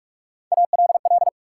Morse code, heard as a keyed tone, about 47 wpm, 700 Hz. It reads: ALL
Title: Sentences from Top 100 Words - New Word Drill - 45wpm